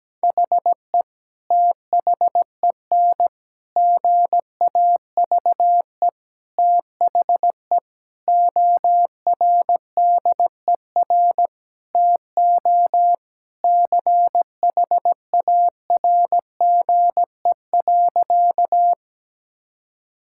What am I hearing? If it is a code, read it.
Morse code, heard as a keyed tone, about 17 wpm, 700 Hz. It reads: HE THEN GAVE THE ORDER TO CHARGE.
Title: Sentences from Top 1000 Words - Review Part 6 - 17wpm